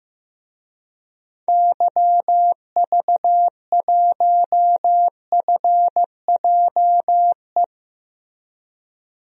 Morse code, heard as a keyed tone, about 15 wpm, 700 Hz. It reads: YV1FJE
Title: DXCC-100 Call Signs - 6 Characters - Repeated 3x - 15wpm